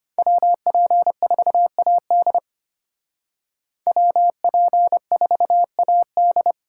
Morse code, heard as a keyed tone, about 32 wpm, 700 Hz. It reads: WP4AD WP4AD